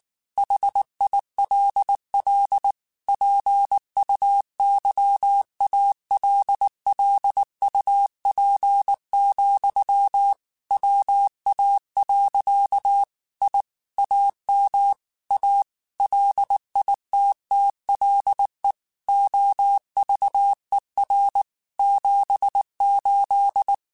Morse code, heard as a keyed tone, about 19 wpm, 800 Hz. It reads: HILL PUYALLUP, WA. I AM A LITTLE OVER 78
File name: QSOs / QSOs-mp3s / kc7sac@18